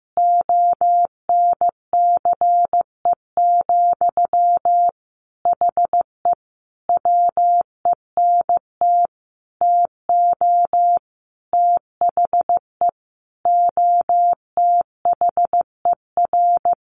Morse code, heard as a keyed tone, about 15 wpm, 700 Hz. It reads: ONCE, HE WENT TO THE OTHER
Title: Sentences from Top 1300 Words 15wpm